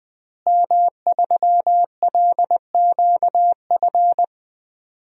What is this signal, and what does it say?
Morse code, heard as a keyed tone, about 20 wpm, 700 Hz. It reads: M3LQF
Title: DXCC-100 Call Signs - 5 Characters - Repeated 2x - 20wpm